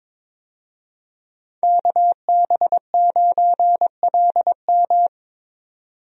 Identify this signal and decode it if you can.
Morse code, heard as a keyed tone, about 22 wpm, 700 Hz. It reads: KB9LM
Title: USA Call Signs - 5 Characters - 22wpm